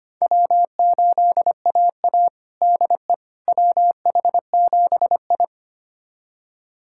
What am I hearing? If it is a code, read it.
Morse code, heard as a keyed tone, about 25 wpm, 700 Hz. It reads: W8AA DE WH7I
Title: USA Callsign DE Callsign - 4 Characters - 25wpm